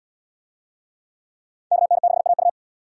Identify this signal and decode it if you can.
Morse code, heard as a keyed tone, about 69 wpm, 700 Hz. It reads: KN6AU